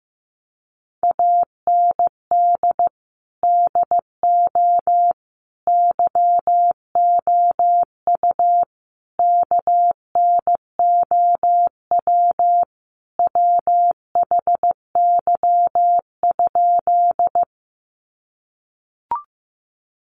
Morse code, heard as a keyed tone, about 15 wpm, 700 Hz. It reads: AND DO YOU KNOW WHY?
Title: Sentences from Top 200 Words 15wpm